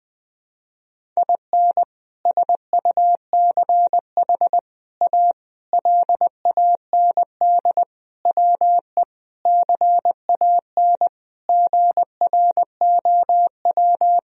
Morse code, heard as a keyed tone, about 20 wpm, 700 Hz. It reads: IN SUCH A LAND WE CAN GROW
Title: Sentences from Top 1000 Words - Review Part 1 - 20wpm